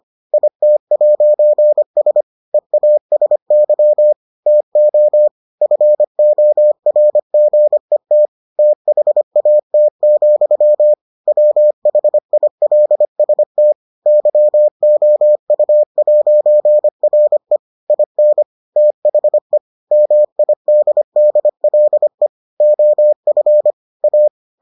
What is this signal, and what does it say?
Morse code, heard as a keyed tone, about 25 wpm, 600 Hz. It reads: IT'S EASY TO FORGET THAT, WHILST YOU'RE IN THE MIDDLE OF A